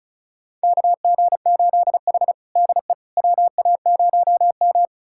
Morse code, heard as a keyed tone, about 36 wpm, 700 Hz. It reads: KG8H DE WA0M
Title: USA Callsign DE Callsign - 4 Characters - 35wpm